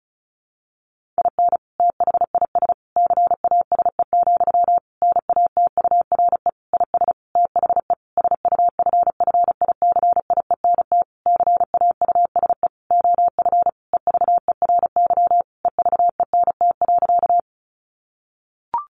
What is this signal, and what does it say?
Morse code, heard as a keyed tone, about 34 wpm, 700 Hz. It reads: IN THIS CASE, NATURE IS THE SUFFICIENT CAUSE OF EVERY EVENT.
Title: Sentences from Top 1400 Words 35wpm